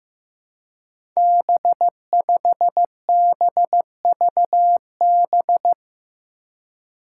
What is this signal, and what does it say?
Morse code, heard as a keyed tone, about 15 wpm, 700 Hz. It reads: B5BVB
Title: Five Challenging Letter-Number Combinations 15wpm